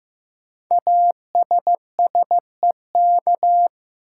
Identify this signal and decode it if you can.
Morse code, heard as a keyed tone, about 15 wpm, 700 Hz. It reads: ASSEK
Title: Code Groups - 5 Letters - 15wpm